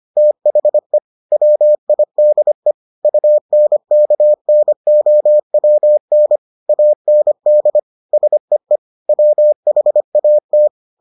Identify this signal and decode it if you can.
Morse code, heard as a keyed tone, about 25 wpm, 600 Hz. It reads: THE WIDE UNKNOWN AND SEE WHAT